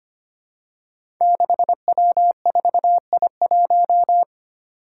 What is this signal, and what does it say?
Morse code, heard as a keyed tone, about 25 wpm, 700 Hz. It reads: 6W4I1
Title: Five Challenging Letter-Number Combinations 25wpm